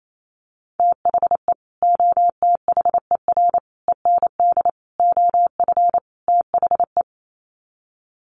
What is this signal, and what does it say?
Morse code, heard as a keyed tone, about 28 wpm, 700 Hz. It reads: THE OTHER END OF THE
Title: Sets of 5 Words 28wpm